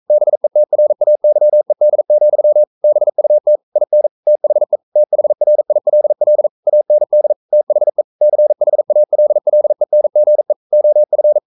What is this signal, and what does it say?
Morse code, heard as a keyed tone, about 42 wpm, 600 Hz. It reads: BETRAYED, BUT IN THE THRILL AND THE CHALLENGE OF
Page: Romanen Never af Ken Follet på engelsk.